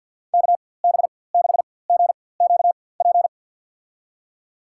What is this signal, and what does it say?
Morse code, heard as a keyed tone, about 53 wpm, spaced as for 24 wpm, 700 Hz. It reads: KB6GQP
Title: USA Call Signs - 6 Characters - 4x Character Spacing - 50wpm